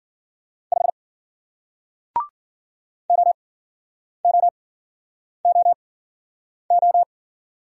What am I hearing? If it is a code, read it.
Morse code, heard as a keyed tone, about 55 wpm, spaced as for 9 wpm, 700 Hz. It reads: 5 OOOO